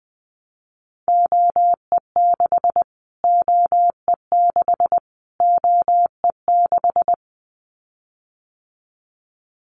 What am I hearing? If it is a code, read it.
Morse code, heard as a keyed tone, about 20 wpm, 700 Hz. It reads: OE6 OE6 OE6
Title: Callsign Prefixes for Top 25 DXCC - Repeated 3x - 20wpm